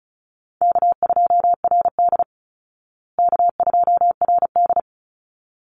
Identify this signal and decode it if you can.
Morse code, heard as a keyed tone, about 34 wpm, 700 Hz. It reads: K2RD K2RD
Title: Easy Call Signs Repeated 2x - 35wpm